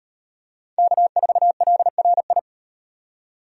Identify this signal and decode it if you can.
Morse code, heard as a keyed tone, about 38 wpm, 700 Hz. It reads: K4LRI